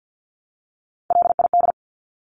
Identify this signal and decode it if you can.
Morse code, heard as a keyed tone, about 50 wpm, 700 Hz. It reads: LID
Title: Abbreviations Repeated 3x - 50wpm